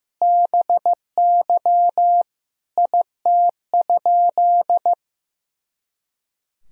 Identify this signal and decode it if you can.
Morse code, heard as a keyed tone, about 16 wpm, 700 Hz. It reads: BY IT?
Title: Sentences from Top 100 Words and Abbreviations 15wpm